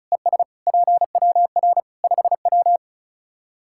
Morse code, heard as a keyed tone, about 36 wpm, 700 Hz. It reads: ES PWR 5W
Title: Practice QSO - Exchange 2 - Info - 35wpm